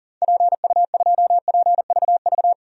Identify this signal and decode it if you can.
Morse code, heard as a keyed tone, about 37 wpm, 700 Hz. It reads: PU2PVV
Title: DXCC-100 Call Signs - 6 Characters - 40wpm